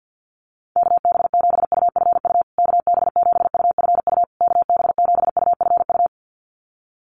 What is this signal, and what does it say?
Morse code, heard as a keyed tone, about 50 wpm, 700 Hz. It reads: KB7UFU KB7UFU KB7UFU
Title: USA Call Signs Repeated 3x 50wpm